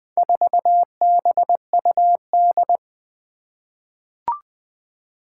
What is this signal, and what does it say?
Morse code, heard as a keyed tone, about 20 wpm, 700 Hz. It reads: 4BUD
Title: Four Challenging Letter-Number Combinations 20wpm